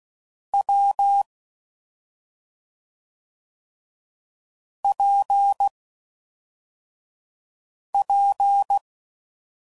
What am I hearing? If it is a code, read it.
Morse code, heard as a keyed tone, about 16 wpm, 800 Hz. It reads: W P P
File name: Combinations / Difficult Combos / AWJ1